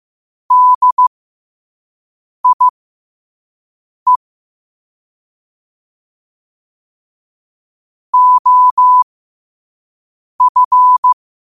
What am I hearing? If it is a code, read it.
Morse code, heard as a keyed tone, about 15 wpm, spaced as for 5 wpm, 1000 Hz. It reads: DIE OF
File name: qotdfiles / 5wpm